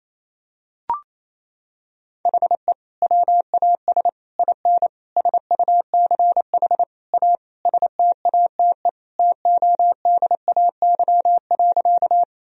Morse code, heard as a keyed tone, about 28 wpm, 700 Hz. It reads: HE WAS IN SUCH A STATE TODAY.
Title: Sentences from Top 1000 Words - Review Part 2 - 28wpm